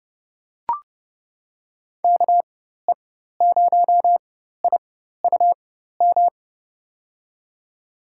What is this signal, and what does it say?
Morse code, heard as a keyed tone, about 31 wpm, spaced as for 14 wpm, 700 Hz. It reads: KE0IUM
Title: USA Call Signs - 6 Characters - 4x Character Spacing - 30wpm